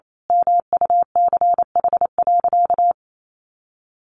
Morse code, heard as a keyed tone, about 28 wpm, 700 Hz. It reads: MUCH.
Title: Sentences from Top 1200 Words 28wpm